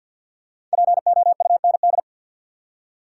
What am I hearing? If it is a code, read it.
Morse code, heard as a keyed tone, about 47 wpm, 700 Hz. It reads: POUND
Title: Five Letter Words - 50wpm